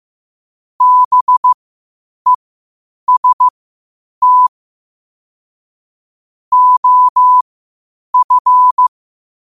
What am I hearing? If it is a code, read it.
Morse code, heard as a keyed tone, about 15 wpm, spaced as for 8 wpm, 1000 Hz. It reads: BEST OF
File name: qotdfiles / 8wpm